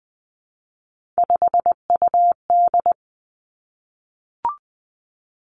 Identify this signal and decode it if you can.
Morse code, heard as a keyed tone, about 20 wpm, 700 Hz. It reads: 5UD
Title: Three Challenging Letter-Number Combinations 20wpm